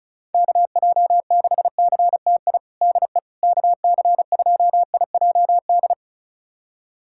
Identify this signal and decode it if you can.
Morse code, heard as a keyed tone, about 36 wpm, 700 Hz. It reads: KJ6CTI DE KC2IJD
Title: USA Callsign DE Callsign - 6 Characters - 35wpm